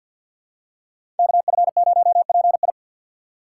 Morse code, heard as a keyed tone, about 50 wpm, 700 Hz. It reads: KF0PI
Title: USA Call Signs - 5 Characters - 2x Character Spacing - 50wpm